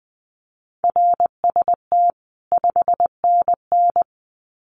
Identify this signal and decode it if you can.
Morse code, heard as a keyed tone, about 20 wpm, 700 Hz. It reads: RST 5NN
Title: Common QSO Phrases 20wpm